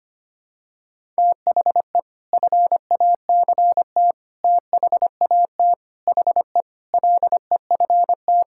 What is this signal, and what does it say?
Morse code, heard as a keyed tone, about 25 wpm, 700 Hz. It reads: THE FACT THAT HE LEFT
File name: practice / 25wpm / Sentences with 2-Word Phrases - 25wpm